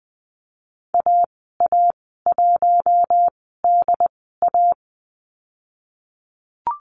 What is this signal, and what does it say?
Morse code, heard as a keyed tone, about 20 wpm, spaced as for 15 wpm, 700 Hz. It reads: AA1DA
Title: 069 Lesson 12g - Letter d - Call Signs 20wpm